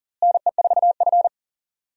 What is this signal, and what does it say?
Morse code, heard as a keyed tone, about 37 wpm, 700 Hz. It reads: NE4F
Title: USA Call Signs - 4 Characters - Repeated 2x - 40wpm